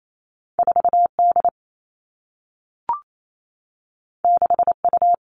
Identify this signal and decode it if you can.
Morse code, heard as a keyed tone, about 28 wpm, 700 Hz. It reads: 4 D 6 U
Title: Two Challenging Letter-Number Combinations 28wpm